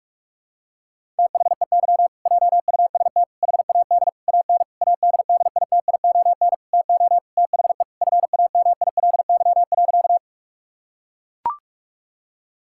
Morse code, heard as a keyed tone, about 43 wpm, 700 Hz. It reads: THEY JUST HAD AN ADDITION TO THE FAMILY.
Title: Sentences from Top 1000 Words - Review Part 6 - 45wpm